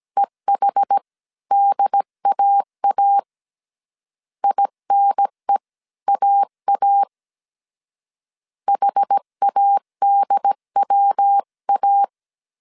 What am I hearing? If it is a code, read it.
Morse code, heard as a keyed tone, about 17 wpm, spaced as for 15 wpm, 800 Hz. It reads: EHDAA INEAA HADWA